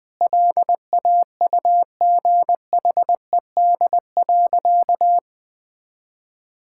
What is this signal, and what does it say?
Morse code, heard as a keyed tone, about 20 wpm, 700 Hz. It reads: LAUGHED.
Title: Sentences with 2-Word Phrases - 20wpm